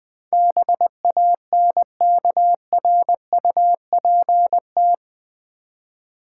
Morse code, heard as a keyed tone, about 20 wpm, 700 Hz. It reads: BANKRUPT
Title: Build A Word - 4 Letters At A Time - 10-Letter Words - 20wpm